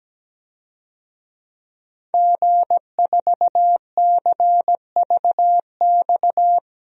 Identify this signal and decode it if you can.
Morse code, heard as a keyed tone, about 17 wpm, 700 Hz. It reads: G4CVX
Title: DXCC-100 Call Signs - 5 Characters - Repeated 2x - 17wpm